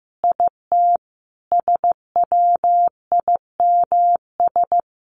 Morse code, heard as a keyed tone, about 15 wpm, 700 Hz. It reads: IT SWIMS
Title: Easy Phrases 15wpm